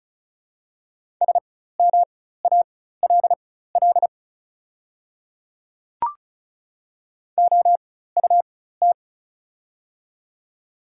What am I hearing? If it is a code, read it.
Morse code, heard as a keyed tone, about 36 wpm, spaced as for 16 wpm, 700 Hz. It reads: SMALL OUT
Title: 4x Character Spacing - Top 200 Words - 35wpm